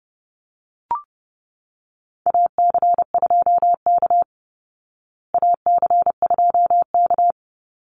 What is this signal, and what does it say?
Morse code, heard as a keyed tone, about 28 wpm, 700 Hz. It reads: AC2K AC2K
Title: Call Signs Repeated 3x - 30wpm